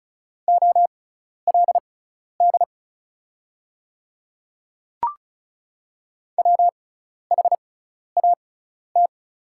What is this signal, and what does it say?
Morse code, heard as a keyed tone, about 36 wpm, spaced as for 12 wpm, 700 Hz. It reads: OLD WHAT
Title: 6x Character Spacing - Top 300 Words - 35wpm